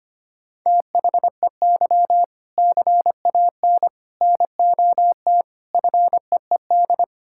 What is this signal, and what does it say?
Morse code, heard as a keyed tone, about 25 wpm, 700 Hz. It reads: THEY CAN NOT FEED
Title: Sentences from Top 1600 Words 25wpm